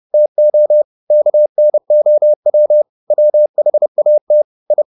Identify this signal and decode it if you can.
Morse code, heard as a keyed tone, about 30 wpm, 600 Hz. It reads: TO KNOW WHAT I
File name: the old man and the sea / the old man and the sea 1 30wpm